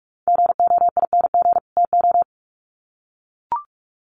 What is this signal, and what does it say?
Morse code, heard as a keyed tone, about 44 wpm, 700 Hz. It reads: GOING TO
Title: Sets of 2 Words 45wpm